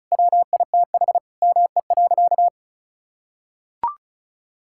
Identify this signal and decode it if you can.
Morse code, heard as a keyed tone, about 34 wpm, 700 Hz. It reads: WITH ME.
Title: Sentences from Top 1000 Words - Review Part 6 - 35wpm